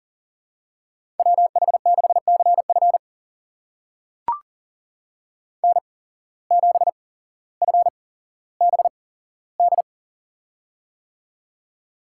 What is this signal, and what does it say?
Morse code, heard as a keyed tone, about 40 wpm, spaced as for 17 wpm, 700 Hz. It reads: WH6CF N7FBD
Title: USA Call Signs - 8x Character Spacing - 40wpm